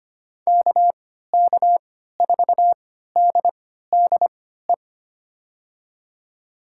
Morse code, heard as a keyed tone, about 25 wpm, spaced as for 14 wpm, 700 Hz. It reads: KK4DDE
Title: USA Call Signs - 6 Characters - 3x Character Spacing - 25wpm